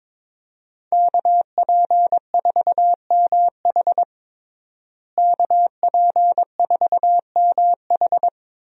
Morse code, heard as a keyed tone, about 22 wpm, 700 Hz. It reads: KP4MH KP4MH
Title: DXCC-100 Call Signs - 5 Characters - Repeated 2x - 22wpm